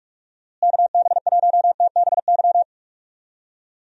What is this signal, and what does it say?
Morse code, heard as a keyed tone, about 41 wpm, 700 Hz. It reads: KB1TBY
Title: USA Call Signs - 6 Characters - Repeated 3x - 45wpm